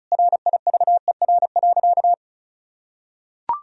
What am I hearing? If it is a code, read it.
Morse code, heard as a keyed tone, about 34 wpm, 700 Hz. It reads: RIVER.
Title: Sentences from Top 1000 Words - Review Part 1 - 35wpm